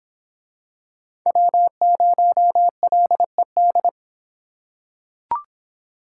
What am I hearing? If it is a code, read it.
Morse code, heard as a keyed tone, about 26 wpm, 700 Hz. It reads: W0LED